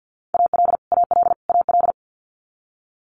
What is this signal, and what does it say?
Morse code, heard as a keyed tone, about 50 wpm, 700 Hz. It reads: AL AL AL
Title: Multi-letter Phonemes Repeated 3x 50wpm